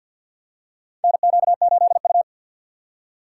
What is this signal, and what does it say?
Morse code, heard as a keyed tone, about 50 wpm, 700 Hz. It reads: NQ8U
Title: USA Call Signs - 4 Characters - 6x Character Spacing - 50wpm